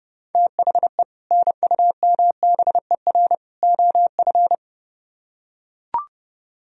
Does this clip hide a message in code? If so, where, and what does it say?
Yes: Morse code from 0.3 to 4.5 s, heard as a keyed tone, about 32 wpm, 700 Hz. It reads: THE NUMBER OF